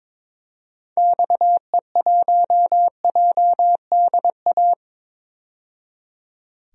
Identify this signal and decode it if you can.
Morse code, heard as a keyed tone, about 22 wpm, 700 Hz. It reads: XE1JDA
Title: DXCC-100 Call Signs - 6 Characters - 22wpm